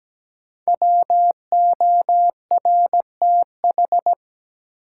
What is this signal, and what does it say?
Morse code, heard as a keyed tone, about 17 wpm, 700 Hz. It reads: WORTH
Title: Sentences from Top 800 Words - New Word Drill - 17wpm